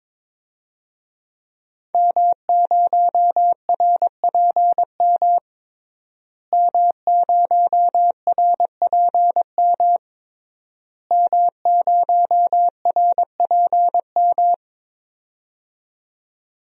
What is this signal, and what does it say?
Morse code, heard as a keyed tone, about 22 wpm, 700 Hz. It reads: M0RPM M0RPM M0RPM
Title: DXCC-100 Call Signs - 5 Characters - Repeated 3x - 22wpm